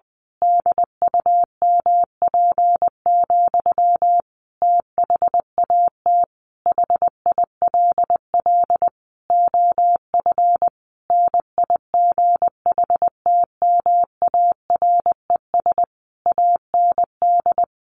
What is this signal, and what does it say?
Morse code, heard as a keyed tone, about 20 wpm, 700 Hz. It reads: DUMP, THAT HILL OF NIGHTMARES AND
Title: A Sound Of Thunder 20wpm